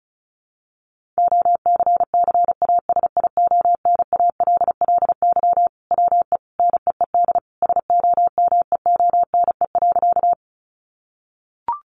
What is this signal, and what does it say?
Morse code, heard as a keyed tone, about 34 wpm, 700 Hz. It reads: OCCASIONALLY WE NEED SOMEONE.
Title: Sentences from Top 1300 Words 35wpm